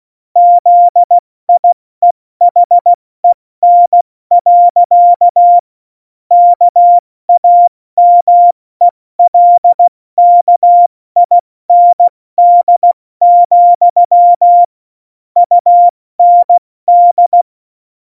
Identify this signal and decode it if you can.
Morse code, heard as a keyed tone, about 16 wpm, 700 Hz. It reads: ZIEHEN. KAMELKIND, UND
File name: Morsekurs / Witze / Witz 06. November 2018, 70 F80